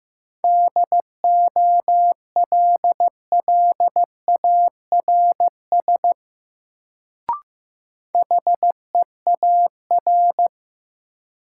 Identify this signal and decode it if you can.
Morse code, heard as a keyed tone, about 15 wpm, 700 Hz. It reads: DOLLARS HEAR
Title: Sets of 5 Words - New Word Drill - 15wpm